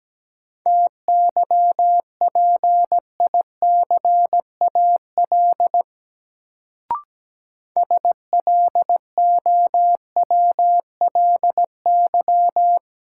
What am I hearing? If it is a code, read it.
Morse code, heard as a keyed tone, about 17 wpm, 700 Hz. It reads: TYPICAL SLOWLY
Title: Sentences from Top 1100 Words - New Word Drill - 17wpm